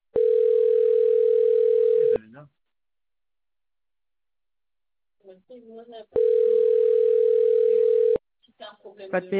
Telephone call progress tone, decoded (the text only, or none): ringback tone